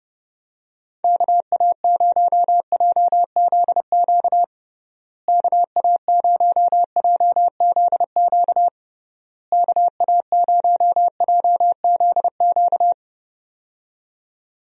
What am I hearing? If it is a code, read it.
Morse code, heard as a keyed tone, about 30 wpm, 700 Hz. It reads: KA0JZQ KA0JZQ KA0JZQ
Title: USA Call Signs - 6 Characters - Repeated 3x - 30wpm